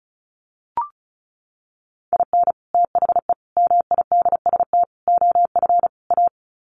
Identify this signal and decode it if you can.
Morse code, heard as a keyed tone, about 34 wpm, 700 Hz. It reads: IN THE MIDST OF A
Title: Sets of 5 Words 35wpm